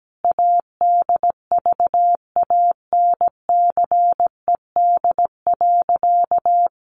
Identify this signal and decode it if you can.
Morse code, heard as a keyed tone, about 17 wpm, 700 Hz. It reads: ADVANCED.
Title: Sentences from Top 1200 Words 17wpm